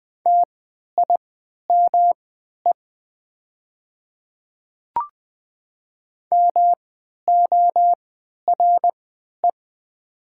Morse code, heard as a keyed tone, about 20 wpm, spaced as for 11 wpm, 690 Hz. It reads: TIME MORE
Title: 3x Character Spacing - Top 100 Words - 20wpm